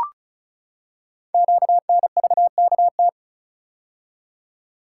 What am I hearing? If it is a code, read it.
Morse code, heard as a keyed tone, about 36 wpm, 700 Hz. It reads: QNVKT
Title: Code Groups - 5 Letter:Numbers - 35wpm